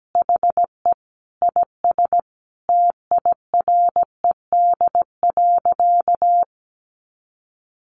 Morse code, heard as a keyed tone, about 17 wpm, 700 Hz. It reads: HE IS TIRED.
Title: Additional Easy Phrases 17wpm